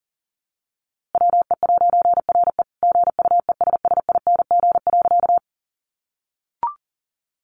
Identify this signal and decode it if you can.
Morse code, heard as a keyed tone, about 40 wpm, 700 Hz. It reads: WE'RE GUESSING.
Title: A Sound Of Thunder 40wpm